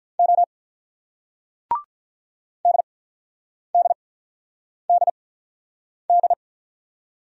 Morse code, heard as a keyed tone, about 44 wpm, spaced as for 9 wpm, 690 Hz. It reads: O DDDD
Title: Speed-Racing Single Letters 35wpm